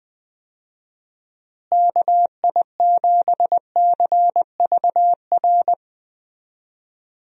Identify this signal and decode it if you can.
Morse code, heard as a keyed tone, about 21 wpm, 700 Hz. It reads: KI7CVR
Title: USA Call Signs - 6 Characters - Repeated 2x - 20wpm